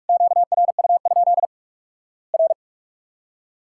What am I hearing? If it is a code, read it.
Morse code, heard as a keyed tone, about 44 wpm, 700 Hz. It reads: QRU? R
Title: Worked All Europe DX Contest - QTC Exchange - 45wpm